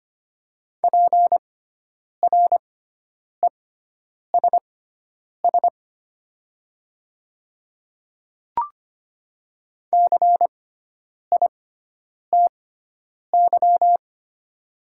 Morse code, heard as a keyed tone, about 25 wpm, spaced as for 9 wpm, 700 Hz. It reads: PRESS CITY
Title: 6x Character Spacing - Top 300 Words - 25wpm